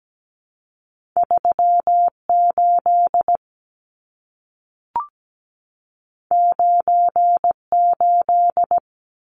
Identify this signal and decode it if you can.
Morse code, heard as a keyed tone, about 17 wpm, 700 Hz. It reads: 38 98
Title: Numbers - 2-Digit 17wpm